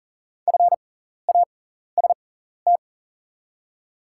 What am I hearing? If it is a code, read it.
Morse code, heard as a keyed tone, about 40 wpm, spaced as for 14 wpm, 690 Hz. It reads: FAST
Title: 6x Character Spacing - Top 500 Words - 40wpm